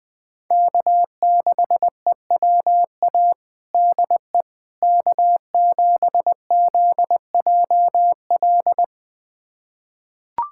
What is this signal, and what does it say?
Morse code, heard as a keyed tone, about 20 wpm, 700 Hz. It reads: K6EWA DE K7ZJL
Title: USA Callsign DE Callsign - 5 Characters - 20wpm